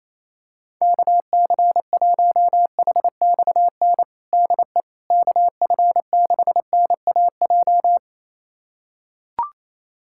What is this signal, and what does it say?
Morse code, heard as a keyed tone, about 27 wpm, 700 Hz. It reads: KC1HXN DE KF6NAJ
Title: USA Callsign DE Callsign - 6 Characters - 28wpm